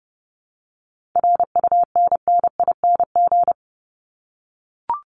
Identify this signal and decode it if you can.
Morse code, heard as a keyed tone, about 30 wpm, 700 Hz. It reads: RUNNING
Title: ING Suffix - Top 100 Words with Base Words 30wpm